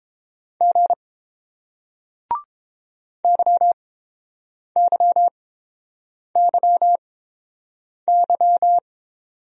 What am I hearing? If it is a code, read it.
Morse code, heard as a keyed tone, about 30 wpm, spaced as for 7 wpm, 700 Hz. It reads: G YYYY